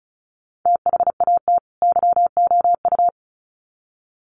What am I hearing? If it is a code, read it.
Morse code, heard as a keyed tone, about 34 wpm, 700 Hz. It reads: THAT YOU
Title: Sets of 2 Words 35wpm